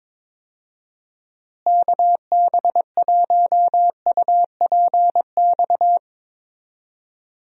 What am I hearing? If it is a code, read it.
Morse code, heard as a keyed tone, about 22 wpm, 700 Hz. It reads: KB1UPX
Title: USA Call Signs - 6 Characters - Repeated 3x - 22wpm